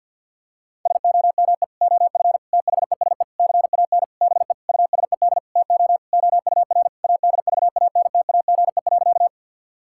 Morse code, heard as a keyed tone, about 50 wpm, 700 Hz. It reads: SOME OF THESE CAN BE USED TO OUR ADVANTAGE.
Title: Sentences from Top 1000 Words - Review Part 6 - 50wpm